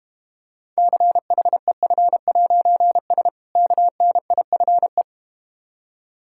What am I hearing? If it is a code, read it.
Morse code, heard as a keyed tone, about 32 wpm, 700 Hz. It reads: CHEF'S KNIFE